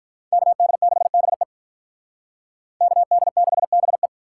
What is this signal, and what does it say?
Morse code, heard as a keyed tone, about 45 wpm, 700 Hz. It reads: KD6BE KD6BE